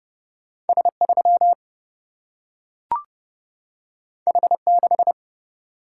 Morse code, heard as a keyed tone, about 30 wpm, 700 Hz. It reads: S 3 H 6
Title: Single-Letter Single-Number 30wpm